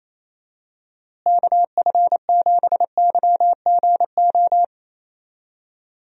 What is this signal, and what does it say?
Morse code, heard as a keyed tone, about 28 wpm, 700 Hz. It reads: KF7YGO
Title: USA Call Signs - 6 Characters - Repeated 2x - 28wpm